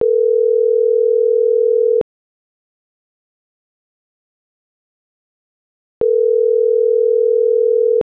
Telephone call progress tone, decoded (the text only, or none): ringback tone